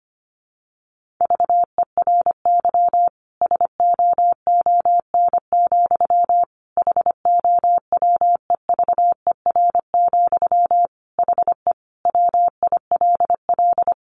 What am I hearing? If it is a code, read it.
Morse code, heard as a keyed tone, about 25 wpm, 700 Hz. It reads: VERY SOON, HOWEVER, HE WILL
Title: Sentences from Top 1400 Words 25wpm